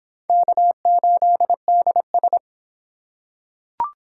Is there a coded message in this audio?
Yes: Morse code from 0.3 to 2.4 s, heard as a keyed tone, about 25 wpm, 700 Hz. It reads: K8DS